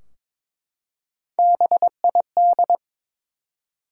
Morse code, heard as a keyed tone, about 22 wpm, 700 Hz. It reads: BID
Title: Three Letter Words - 22wpm